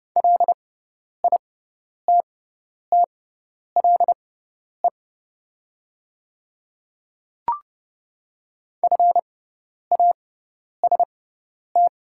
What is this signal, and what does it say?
Morse code, heard as a keyed tone, about 31 wpm, spaced as for 10 wpm, 700 Hz. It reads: LITTLE FAST
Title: 6x Character Spacing - Top 500 Words - 30wpm